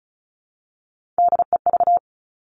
Morse code, heard as a keyed tone, about 36 wpm, 700 Hz. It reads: DEV
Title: Three Challenging Letter-Number Combinations 35wpm